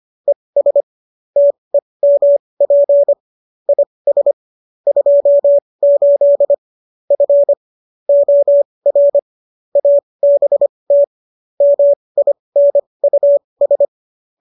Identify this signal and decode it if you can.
Morse code, heard as a keyed tone, about 25 wpm, spaced as for 20 wpm, 570 Hz. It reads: ES TEMP IS 28 F OR ABT MINUS